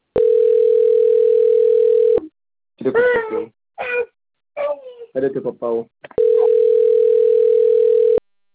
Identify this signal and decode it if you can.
Ringback tone (North America), 440 + 480 Hz, with speech.
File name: rg-1600-unknown- - - .73909